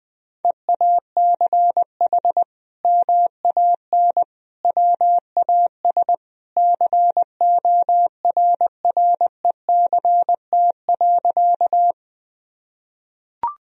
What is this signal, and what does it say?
Morse code, heard as a keyed tone, about 21 wpm, 700 Hz. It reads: EACH MAN WAS CORRECT.
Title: Sentences from Top 1000 Words - Review Part 2 - 20wpm